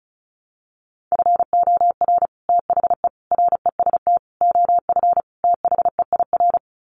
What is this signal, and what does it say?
Morse code, heard as a keyed tone, about 34 wpm, 700 Hz. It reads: FOR THE REST OF THEIR
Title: Sets of 5 Words Encore2 35wpm